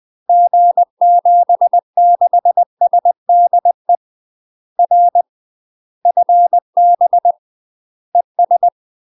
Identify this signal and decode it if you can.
Morse code, heard as a keyed tone, about 20 wpm, 700 Hz. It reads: G76SDE R FB ES
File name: UK MASTER QSOs TRAINING/20WPM